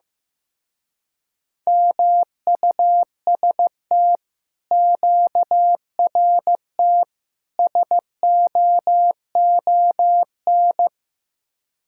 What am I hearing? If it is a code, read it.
Morse code, heard as a keyed tone, about 16 wpm, 700 Hz. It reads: MUST QRT SOON
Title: Speed-Racing Common QSO Phrases Encore - 15wpm